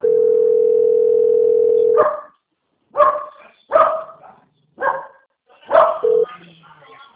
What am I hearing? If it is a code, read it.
Ringback tone (North America), 440 + 480 Hz.
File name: out- - - - - .72609